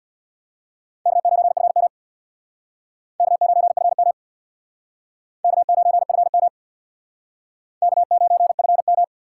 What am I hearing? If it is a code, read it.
Morse code, heard as a keyed tone, about 60 wpm, 700 Hz. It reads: K9FM K9FM K9FM K9FM
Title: Speed Racing - USA 1x2 Call Signs 50wpm